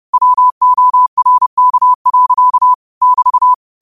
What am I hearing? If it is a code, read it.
Morse code, heard as a keyed tone, about 30 wpm, 1000 Hz. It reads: WORK. <BT>